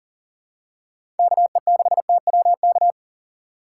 Morse code, heard as a keyed tone, about 40 wpm, 700 Hz. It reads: KE6TWK
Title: USA Call Signs - 6 Characters - 2x Character Spacing - 40wpm